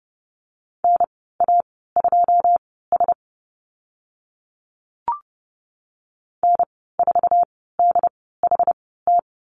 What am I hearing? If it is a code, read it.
Morse code, heard as a keyed tone, about 30 wpm, spaced as for 17 wpm, 700 Hz. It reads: NA2S N4DHT
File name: Lesson 14g - Number 2 - Call Signs 30wpm